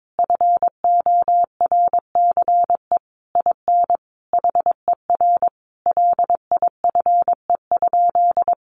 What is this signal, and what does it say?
Morse code, heard as a keyed tone, about 22 wpm, 700 Hz. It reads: FORCE IN HER LIFE?
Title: Sentences from Top 1500 Words 22wpm